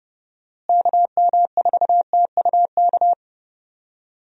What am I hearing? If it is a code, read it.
Morse code, heard as a keyed tone, about 29 wpm, 700 Hz. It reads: KM4TUK
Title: USA Call Signs - 6 Characters - Repeated 3x - 30wpm